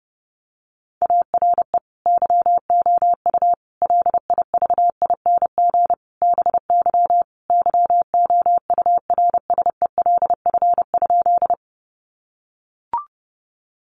Morse code, heard as a keyed tone, about 30 wpm, 700 Hz. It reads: ARE YOU LIVING BY YOURSELF?
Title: Sentences from Top 1100 Words 30wpm